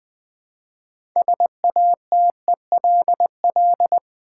Morse code, heard as a keyed tone, about 20 wpm, 700 Hz. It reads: SATELL
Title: Build A Word - 3 Letters At A Time - 9-Letter Words - 20wpm